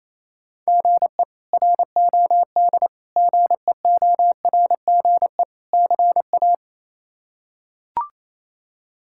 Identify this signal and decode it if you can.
Morse code, heard as a keyed tone, about 27 wpm, 700 Hz. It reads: GE ROD GEORGE CA
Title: K1USN SST QSO Practice - 28wpm